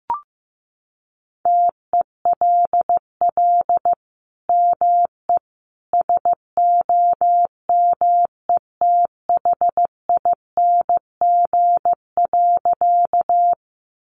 Morse code, heard as a keyed tone, about 15 wpm, 700 Hz. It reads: TELL ME SOMETHING.
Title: Sentences from Top 600 Words 15wpm